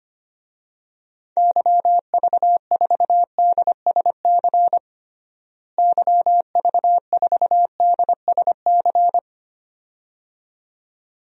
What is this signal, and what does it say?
Morse code, heard as a keyed tone, about 25 wpm, 700 Hz. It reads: YV4DSC YV4DSC
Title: DXCC-100 Call Signs - 6 Characters - Repeated 2x - 25wpm